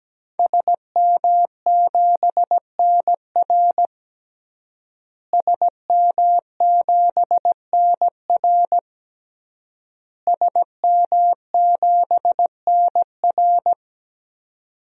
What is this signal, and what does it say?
Morse code, heard as a keyed tone, about 17 wpm, 700 Hz. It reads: SM7NR SM7NR SM7NR
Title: DXCC-100 Call Signs - 5 Characters - Repeated 3x - 17wpm